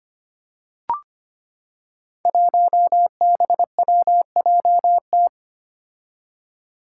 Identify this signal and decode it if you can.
Morse code, heard as a keyed tone, about 25 wpm, 700 Hz. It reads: 1BWJT
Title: Five Challenging Letter-Number Combinations 25wpm